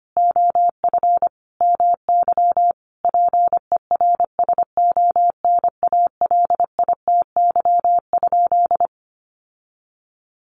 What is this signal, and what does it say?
Morse code, heard as a keyed tone, about 25 wpm, 700 Hz. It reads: OF MY PERSONALITY?
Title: Sentences from Top 1400 Words 25wpm